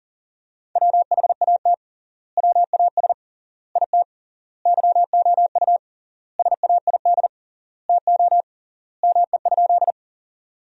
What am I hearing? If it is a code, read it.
Morse code, heard as a keyed tone, about 40 wpm, 700 Hz. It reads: WHAT WAS IT YOU SAID TO ME?
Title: 3x Word-Spacing - Sentences from Top 200 Words - 40wpm